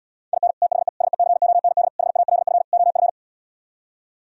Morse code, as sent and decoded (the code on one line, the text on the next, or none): .. -. / - .... / ... -.-. --- -. -.. / .... .- .-.. ..-. / --- ..-.
IN THE SECOND HALF OF